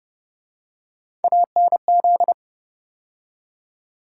Morse code, heard as a keyed tone, about 29 wpm, 700 Hz. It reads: ANZ